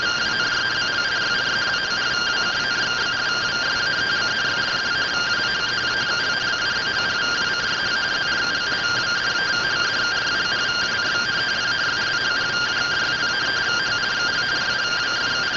Two-tone teleprinter signal, about 170 Hz shift.